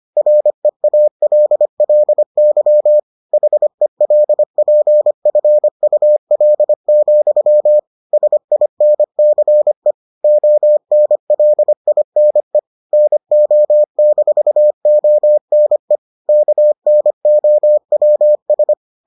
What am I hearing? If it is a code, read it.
Morse code, heard as a keyed tone, about 25 wpm, 600 Hz. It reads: REALLY HELPFUL, SINCE ONLINE NO-ONE KNOWS